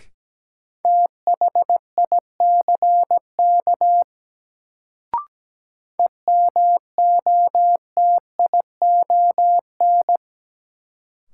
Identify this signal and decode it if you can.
Morse code, heard as a keyed tone, about 17 wpm, 700 Hz. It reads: THICK EMOTION
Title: Sentences from Top 1600 Words - New Word Drill Alt - 17wpm